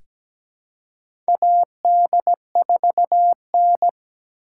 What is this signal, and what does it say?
Morse code, heard as a keyed tone, about 17 wpm, 700 Hz. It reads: AD4N
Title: USA Call Signs - 4 Characters - 4x Character Spacing - 17wpm